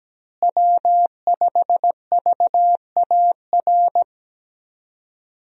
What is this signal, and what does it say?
Morse code, heard as a keyed tone, about 17 wpm, 700 Hz. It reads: W5VAR
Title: USA Call Signs - 5 Characters - Repeated 3x - 17wpm